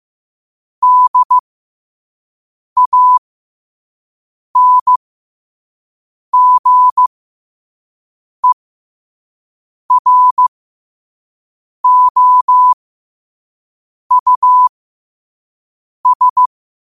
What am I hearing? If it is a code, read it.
Morse code, heard as a keyed tone, about 15 wpm, spaced as for 5 wpm, 1000 Hz. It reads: DANGEROUS
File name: qotdfiles / 5wpm